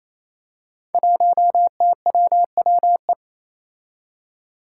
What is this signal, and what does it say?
Morse code, heard as a keyed tone, about 28 wpm, 700 Hz. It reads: 1TWWE
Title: Five Challenging Letter-Number Combinations 28wpm